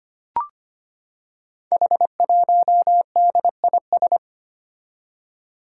Morse code, heard as a keyed tone, about 25 wpm, 700 Hz. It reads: H1DIS
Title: Five Challenging Letter-Number Combinations 25wpm